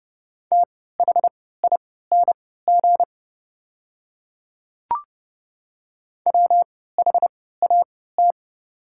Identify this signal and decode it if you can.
Morse code, heard as a keyed tone, about 30 wpm, spaced as for 17 wpm, 700 Hz. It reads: THING WHAT
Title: 3x Character Spacing - Top 200 Words - 30wpm